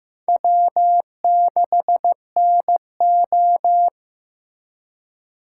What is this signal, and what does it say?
Morse code, heard as a keyed tone, about 15 wpm, 700 Hz. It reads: W6NO
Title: USA Call Signs - 4 Characters - 2x Character Spacing - 15wpm